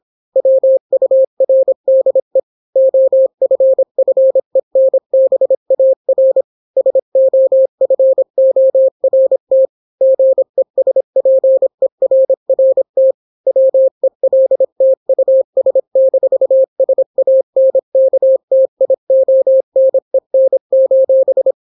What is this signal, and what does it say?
Morse code, heard as a keyed tone, about 26 wpm, 530 Hz. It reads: WURDE OFFENBAR SOFORT GESPERRT WELT US-SANKTIONEN: